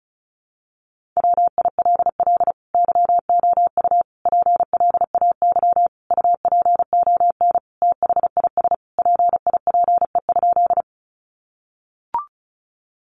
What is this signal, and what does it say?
Morse code, heard as a keyed tone, about 34 wpm, 700 Hz. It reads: WILL YOU PLAY UPON THIS PIPE?
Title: Sentences from Top 1500 Words 35wpm